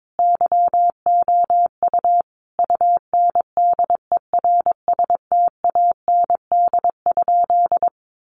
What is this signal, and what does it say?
Morse code, heard as a keyed tone, about 22 wpm, 700 Hz. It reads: YOU UNDERSTAND?
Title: Sentences from Top 1500 Words 22wpm